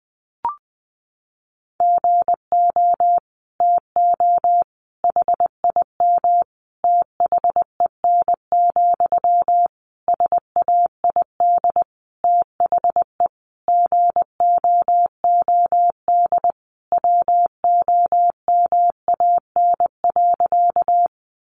Morse code, heard as a keyed tone, about 20 wpm, 700 Hz. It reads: GO TO HIM THEN, SAID THE GOOD WOMAN.
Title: Sentences from Top 1100 Words 20wpm